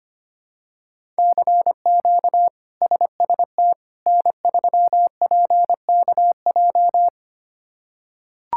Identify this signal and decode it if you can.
Morse code, heard as a keyed tone, about 25 wpm, 700 Hz. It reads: CQ SST N3PKJ
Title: K1USN SST QSO Practice - 25wpm